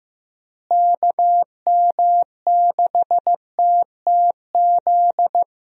Morse code, heard as a keyed tone, about 15 wpm, 700 Hz. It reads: KM6TTZ
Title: USA Call Signs - 6 Characters - Repeated 2x - 15wpm